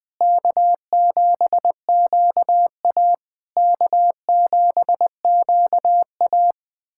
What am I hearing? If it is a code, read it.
Morse code, heard as a keyed tone, about 20 wpm, 700 Hz. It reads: K7QA K7QA
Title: SOTA QSO Practice 20wpm